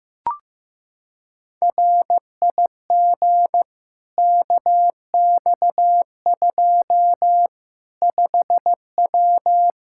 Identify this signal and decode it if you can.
Morse code, heard as a keyed tone, about 15 wpm, 700 Hz. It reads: RIG KX2 5W
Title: Practice QSO - Exchange 2 - Info - 15wpm